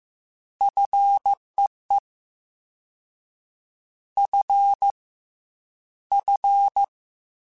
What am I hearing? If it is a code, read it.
Morse code, heard as a keyed tone, about 15 wpm, 800 Hz. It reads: FEE F F
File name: Combinations / Combinations-mp3s / Combinations@13